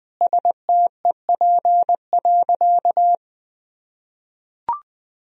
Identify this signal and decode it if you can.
Morse code, heard as a keyed tone, about 21 wpm, 700 Hz. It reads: STEP.
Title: Sentences from Top 1000 Words - Review Part 2 - 20wpm